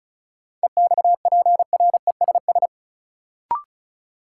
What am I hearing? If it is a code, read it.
Morse code, heard as a keyed tone, about 34 wpm, 700 Hz. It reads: EXPRESS
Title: Sentences from Top 1100 Words - New Word Drill - 35wpm